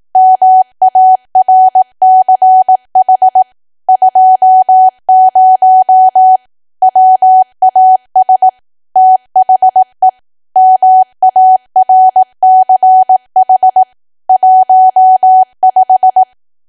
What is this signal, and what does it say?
Morse code, heard as a keyed tone, about 18 wpm, 750 Hz. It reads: MARCH 20 WAS THE MARCH 15
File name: cwa / W1AW-Level2 / 18WPM